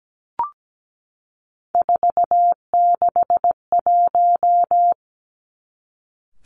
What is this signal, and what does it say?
Morse code, heard as a keyed tone, about 17 wpm, 700 Hz. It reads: 461
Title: Three Challenging Letter-Number Combinations 17wpm